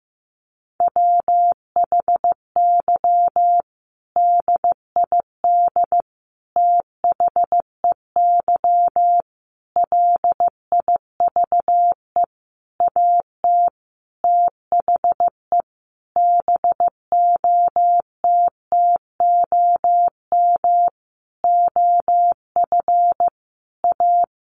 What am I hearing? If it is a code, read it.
Morse code, heard as a keyed tone, about 15 wpm, 700 Hz. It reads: WHY DID THEY LIVE AT THE BOTTOM OF A
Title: Sentences from Top 900 Words 15wpm